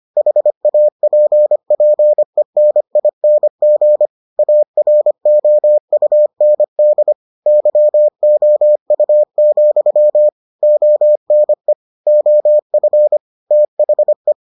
Morse code, heard as a keyed tone, about 25 wpm, 600 Hz. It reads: HAPPENING AROUND YOU, ONE OF THE